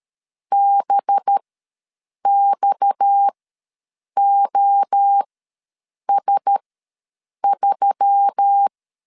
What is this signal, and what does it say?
Morse code, heard as a keyed tone, about 13 wpm, spaced as for 7 wpm, 800 Hz. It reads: BXOS3
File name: LEZIONI / 6 PKX0 / 6 PKX ws 12wpm 3106